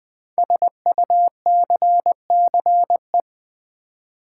Morse code, heard as a keyed tone, about 20 wpm, 700 Hz. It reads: SUCCE
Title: Build A Word - 5 Letters At A Time - 12-Letter Words - 20wpm